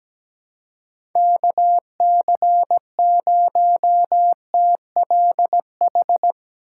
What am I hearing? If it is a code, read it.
Morse code, heard as a keyed tone, about 17 wpm, 700 Hz. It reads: KC0TLH
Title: USA Call Signs - 4x Character Spacing - 17wpm